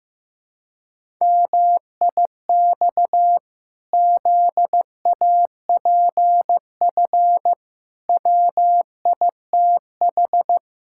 Morse code, heard as a keyed tone, about 14 wpm, 700 Hz. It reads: MIX ZAPF WITH
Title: Pangram Challenge v2 - 15wpm